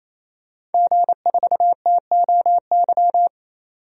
Morse code, heard as a keyed tone, about 28 wpm, 700 Hz. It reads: G4TOY
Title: DXCC-100 Call Signs - 5 Characters - 28wpm